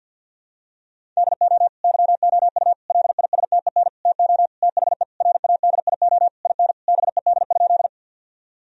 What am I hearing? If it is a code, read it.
Morse code, heard as a keyed tone, about 51 wpm, 700 Hz. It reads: DO YOU LISTEN TO THE RADIO IN BED?
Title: Sentences from Top 1000 Words - Review Part 3 - 50wpm